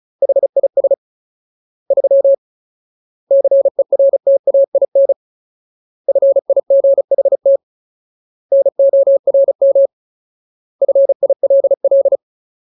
Morse code, heard as a keyed tone, about 35 wpm, 550 Hz. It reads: HIS 3 CERTAIN FIGHT NORM FILL
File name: Ultimate Word List / 35WPM Ultimate